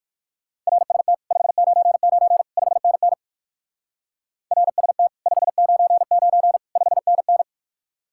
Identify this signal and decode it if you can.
Morse code, heard as a keyed tone, about 56 wpm, 700 Hz. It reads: RST 599 5NN RST 599 5NN
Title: Speed-Racing Common QSO Phrases - 45wpm